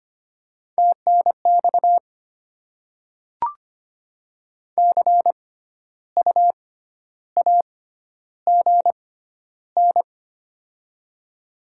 Morse code, heard as a keyed tone, about 25 wpm, spaced as for 12 wpm, 700 Hz. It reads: TNX CU AGN
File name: Top 20 QSO Elements - 6x Character Spacing - 25wpm